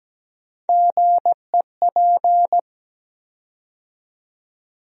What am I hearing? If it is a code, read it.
Morse code, heard as a keyed tone, about 17 wpm, 700 Hz. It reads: GEP
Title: USA Call Sign Postfixes - 3 Letters 17wpm